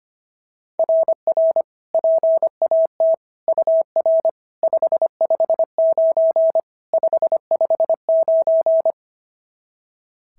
Morse code, heard as a keyed tone, about 25 wpm, 690 Hz. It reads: RR PAT UR 559 559
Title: SOTA QSO Practice 25wpm